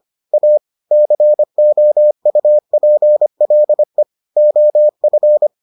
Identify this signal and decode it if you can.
Morse code, heard as a keyed tone, about 25 wpm, 600 Hz. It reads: A COUPLE OF